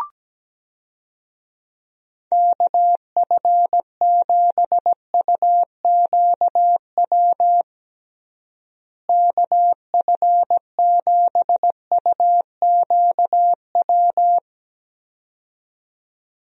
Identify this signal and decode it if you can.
Morse code, heard as a keyed tone, about 17 wpm, 700 Hz. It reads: KF7UQW KF7UQW
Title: USA Call Signs - 6 Characters - Repeated 2x - 17wpm